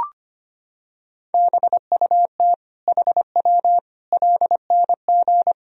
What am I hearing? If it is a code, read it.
Morse code, heard as a keyed tone, about 25 wpm, 700 Hz. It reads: BUT HW LNG
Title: Sentences from Top 500 Words and Abbreviations 25wpm